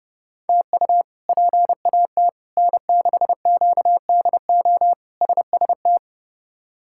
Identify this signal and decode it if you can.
Morse code, heard as a keyed tone, about 30 wpm, 700 Hz. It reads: TU PAT N6QDO SST
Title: K1USN SST QSO Practice - 30wpm